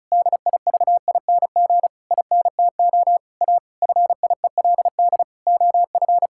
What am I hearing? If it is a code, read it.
Morse code, heard as a keyed tone, about 36 wpm, 700 Hz. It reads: DIVING INTO A FIELD OF
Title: Pangram Challenge v2 - 35wpm